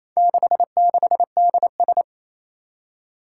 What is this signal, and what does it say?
Morse code, heard as a keyed tone, about 27 wpm, 700 Hz. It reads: 66DS
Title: Four Challenging Letter-Number Combinations 28wpm